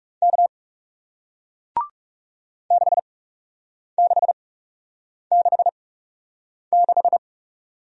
Morse code, heard as a keyed tone, about 41 wpm, spaced as for 8 wpm, 700 Hz. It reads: K 6666